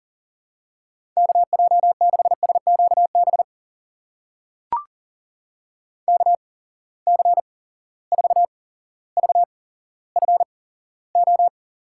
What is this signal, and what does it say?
Morse code, heard as a keyed tone, about 40 wpm, spaced as for 17 wpm, 700 Hz. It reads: KJ6SQB KC4VFO
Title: USA Call Signs - 6 Characters - 8x Character Spacing - 40wpm